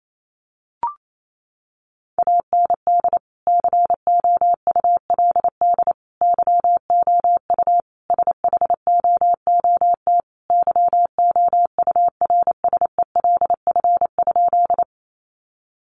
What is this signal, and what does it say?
Morse code, heard as a keyed tone, about 28 wpm, 700 Hz. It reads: AND COULD YOU SHOOT YOURSELF?
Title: Sentences from Top 1200 Words 28wpm